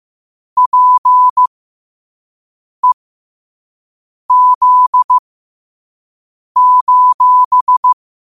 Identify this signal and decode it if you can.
Morse code, heard as a keyed tone, about 15 wpm, spaced as for 5 wpm, 1000 Hz. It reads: PEZ:
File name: qotdfiles / 5wpm